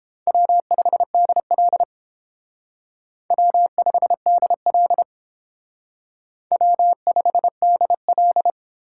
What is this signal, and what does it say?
Morse code, heard as a keyed tone, about 28 wpm, 700 Hz. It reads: W5DL W5DL W5DL